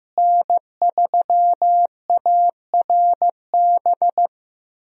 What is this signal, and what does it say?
Morse code, heard as a keyed tone, about 15 wpm, 700 Hz. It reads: N3ARB
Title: USA Call Signs - 5 Characters - Repeated 2x - 15wpm